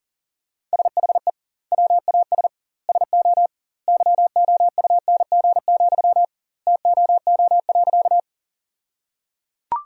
Morse code, heard as a keyed tone, about 37 wpm, 700 Hz. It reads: SHE WAS SO YOUNG, TOO.
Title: 2x Word-Spacing - Sentences from Top 400 Words - 40wpm